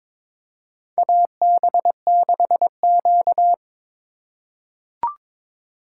Morse code, heard as a keyed tone, about 22 wpm, 700 Hz. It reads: AB6Q
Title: USA Call Signs - 4 Characters - 4x Character Spacing - 22wpm